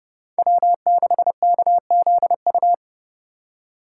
Morse code, heard as a keyed tone, about 30 wpm, 700 Hz. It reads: W6KZU
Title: USA Call Signs - 5 Characters - 4x Character Spacing - 30wpm